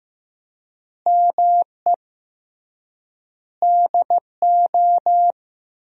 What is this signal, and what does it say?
Morse code, heard as a keyed tone, about 15 wpm, 700 Hz. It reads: ME DO
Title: 3x Word-Spacing - Sentences from Top 300 Words - 15wpm